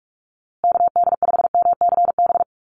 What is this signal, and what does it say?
Morse code, heard as a keyed tone, about 44 wpm, 690 Hz. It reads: KD5MCB
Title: USA Call Signs Repeated 3x 45wpm